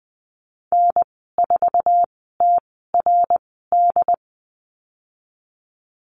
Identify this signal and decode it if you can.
Morse code, heard as a keyed tone, about 20 wpm, spaced as for 15 wpm, 700 Hz. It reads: N4TRD
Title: 069 Lesson 12g - Letter d - Call Signs 20wpm